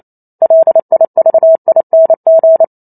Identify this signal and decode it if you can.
Morse code, heard as a keyed tone, about 28 wpm, 650 Hz. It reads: LIVING